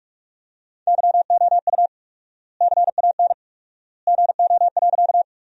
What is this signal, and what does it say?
Morse code, heard as a keyed tone, about 46 wpm, 700 Hz. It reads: YOU CAN GO.
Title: 4x Word-Spacing - Sentences from Top 100 Words - 45wpm